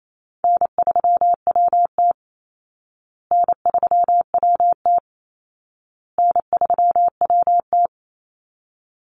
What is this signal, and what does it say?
Morse code, heard as a keyed tone, about 28 wpm, 700 Hz. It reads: N3WT N3WT N3WT
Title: Easy Call Signs Repeated 3x - 28wpm